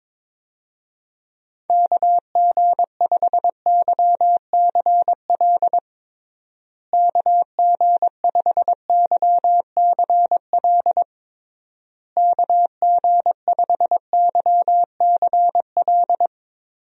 Morse code, heard as a keyed tone, about 22 wpm, 700 Hz. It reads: KG5YCL KG5YCL KG5YCL
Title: USA Call Signs - 6 Characters - Repeated 3x - 22wpm